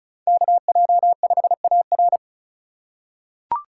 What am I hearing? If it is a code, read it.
Morse code, heard as a keyed tone, about 36 wpm, 700 Hz. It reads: KJ5AR
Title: USA Call Signs - 2x Character Spacing - 35wpm